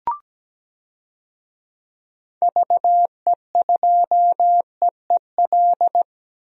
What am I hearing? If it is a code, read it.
Morse code, heard as a keyed tone, about 17 wpm, 700 Hz. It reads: VE2EEL
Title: DXCC-100 Call Signs - 6 Characters - Repeated 3x - 17wpm